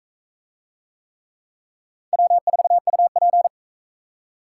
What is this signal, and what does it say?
Morse code, heard as a keyed tone, about 42 wpm, 700 Hz. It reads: W4UP